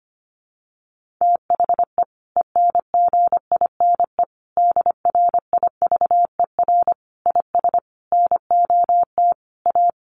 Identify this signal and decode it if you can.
Morse code, heard as a keyed tone, about 25 wpm, 700 Hz. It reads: THE ENGINE DRIVER IS NOT A
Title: Sentences from Top 1500 Words 25wpm